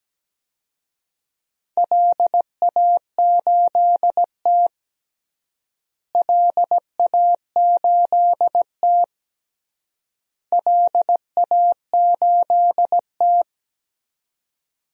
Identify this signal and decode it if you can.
Morse code, heard as a keyed tone, about 17 wpm, 700 Hz. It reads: LA8T LA8T LA8T
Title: DXCC-100 Call Signs - 4 Characters - Repeated 3x - 17wpm